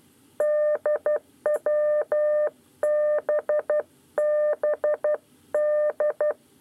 Morse code, heard as a keyed tone, about 12 wpm, 600 Hz. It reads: DWBBD
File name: 9 lezione cw 2